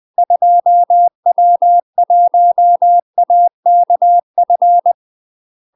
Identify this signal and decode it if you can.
Morse code, heard as a keyed tone, about 20 wpm, 700 Hz. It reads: 2W1AKF
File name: UK MASTER QSOs TRAINING / 20WPM / UK MASTER QSOs 0047